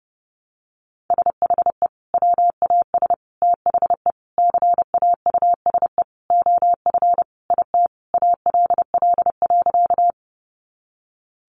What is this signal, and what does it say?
Morse code, heard as a keyed tone, about 30 wpm, 700 Hz. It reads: SHE WAS THE CAUSE OF IT ALL.
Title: Sentences from Top 200 Words 30wpm